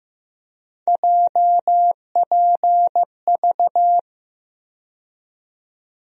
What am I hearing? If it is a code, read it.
Morse code, heard as a keyed tone, about 15 wpm, 700 Hz. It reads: JPV
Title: USA Call Sign Postfixes - 3 Letters 15wpm